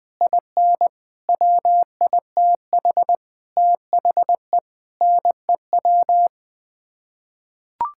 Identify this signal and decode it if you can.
Morse code, heard as a keyed tone, about 21 wpm, 700 Hz. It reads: IN WITH THE NEW
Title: Sets of 9 Words - 20wpm